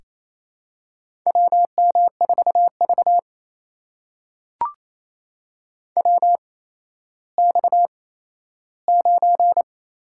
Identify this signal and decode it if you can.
Morse code, heard as a keyed tone, about 28 wpm, 700 Hz. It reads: W M 4 V W X 9
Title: USA Call Signs - 4 Characters - 8x Character Spacing - 28wpm